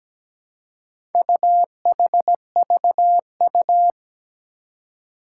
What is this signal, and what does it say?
Morse code, heard as a keyed tone, about 17 wpm, 700 Hz. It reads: UHVU
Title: Four Challenging Letter-Number Combinations 17wpm